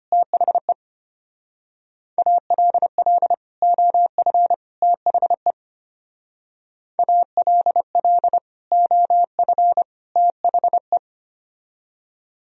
Morse code, heard as a keyed tone, about 31 wpm, 700 Hz. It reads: THE ALL OF THE ALL OF THE
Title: Speed-Racing - Sets of 3 Words 25wpm